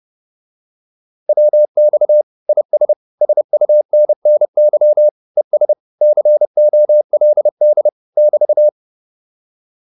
Morse code, heard as a keyed tone, about 30 wpm, 600 Hz. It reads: WX IS SUNNY ES COLD <BT>